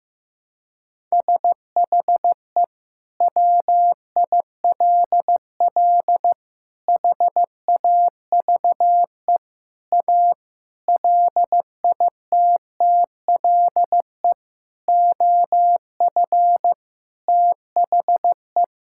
Morse code, heard as a keyed tone, about 15 wpm, 700 Hz. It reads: SHE WILL HAVE A LITTLE OF THE
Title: Sentences from Top 1000 Words - Review Part 5 - 15wpm